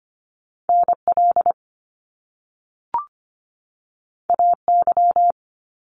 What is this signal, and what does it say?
Morse code, heard as a keyed tone, about 25 wpm, 700 Hz. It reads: NL AY
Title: Top Two Letter Combinations in Words 25wpm